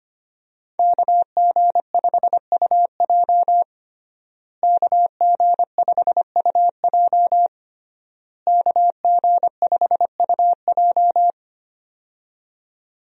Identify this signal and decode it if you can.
Morse code, heard as a keyed tone, about 25 wpm, 700 Hz. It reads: KG5UJ KG5UJ KG5UJ
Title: USA Call Signs - 5 Characters - Repeated 3x - 25wpm